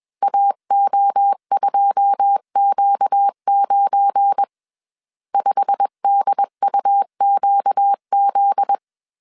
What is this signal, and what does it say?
Morse code, heard as a keyed tone, about 21 wpm, 800 Hz. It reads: AO2Q9 5DUQZ
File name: CW / DET / 1 9 1500C S C 20wpm 2310